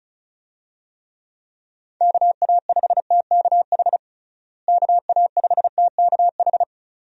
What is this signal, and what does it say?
Morse code, heard as a keyed tone, about 36 wpm, 700 Hz. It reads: KA5TKH KA5TKH
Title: USA Call Signs - 6 Characters - Repeated 2x - 35wpm